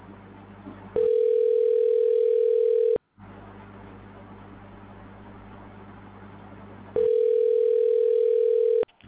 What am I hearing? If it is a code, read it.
Ringback tone (North America), 440 + 480 Hz.